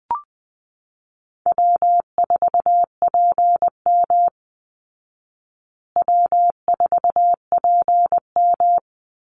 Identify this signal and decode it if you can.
Morse code, heard as a keyed tone, about 20 wpm, 700 Hz. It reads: W4PM W4PM
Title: Call Signs Repeated 3x - 20wpm